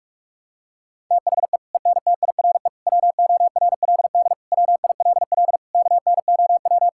Morse code, heard as a keyed tone, about 46 wpm, 690 Hz. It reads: THE ENTIRE WORLD WILL KNOW
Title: Sentences from Top 1000 Words - Review Part 5 - 45wpm